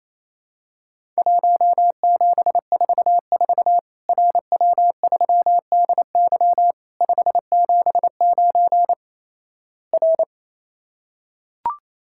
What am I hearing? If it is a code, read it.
Morse code, heard as a keyed tone, about 28 wpm, 700 Hz. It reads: 1744 RW3DY 579 R
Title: Worked All Europe DX Contest - QTC Exchange - 28wpm